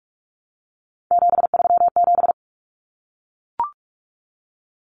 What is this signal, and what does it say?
Morse code, heard as a keyed tone, about 44 wpm, 700 Hz. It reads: 737
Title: Numbers - 3-Digit 45wpm